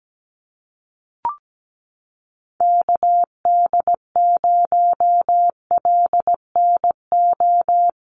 Morse code, heard as a keyed tone, about 17 wpm, 700 Hz. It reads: KD0LNO
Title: USA Call Signs Repeated 3x 17wpm